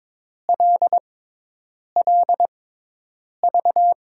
Morse code, heard as a keyed tone, about 22 wpm, spaced as for 8 wpm, 700 Hz. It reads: LLV
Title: USA Call Signs - 6 Characters - 6x Character Spacing - 22wpm